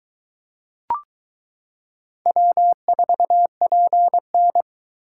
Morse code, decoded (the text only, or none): W4PN